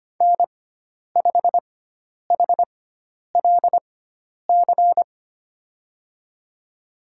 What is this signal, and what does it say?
Morse code, heard as a keyed tone, about 25 wpm, spaced as for 10 wpm, 700 Hz. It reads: N5HLC
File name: Lesson 16g - Letter c - Call Signs 25wpm